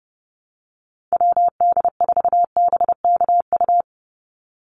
Morse code, heard as a keyed tone, about 30 wpm, 700 Hz. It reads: WD4BKU
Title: USA Call Signs 30wpm